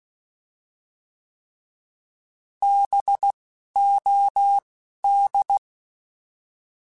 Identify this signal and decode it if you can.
Morse code, heard as a keyed tone, about 16 wpm, spaced as for 11 wpm, 800 Hz. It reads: BOD
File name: Alphabetical / Alphabetical mp3s / Bs @10